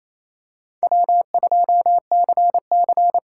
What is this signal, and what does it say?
Morse code, heard as a keyed tone, about 28 wpm, 700 Hz. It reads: W2CC
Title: USA Callsign DE Callsign - 4 Characters - 28wpm